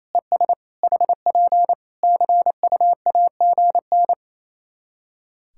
Morse code, heard as a keyed tone, about 28 wpm, 700 Hz. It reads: ES HP CUAGN
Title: Practice QSO - Exchange 4 - Ending - 28wpm